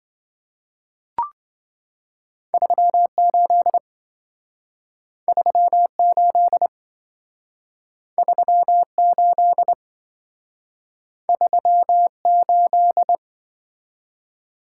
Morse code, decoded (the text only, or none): 38 38 38 38